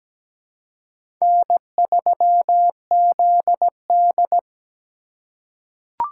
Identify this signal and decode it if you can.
Morse code, heard as a keyed tone, about 17 wpm, 700 Hz. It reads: N3ZD
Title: USA Call Signs - 4 Characters - 2x Character Spacing - 17wpm